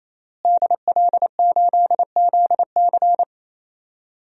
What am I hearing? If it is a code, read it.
Morse code, heard as a keyed tone, about 28 wpm, 700 Hz. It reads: DL8ZC
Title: DXCC-100 Call Signs - 5 Characters - Repeated 3x - 28wpm